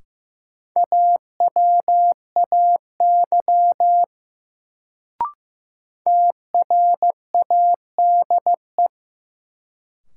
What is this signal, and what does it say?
Morse code, heard as a keyed tone, about 16 wpm, 700 Hz. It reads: AWAY TRADE
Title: Sentences from Top 600 Words - New Word Drill - 15wpm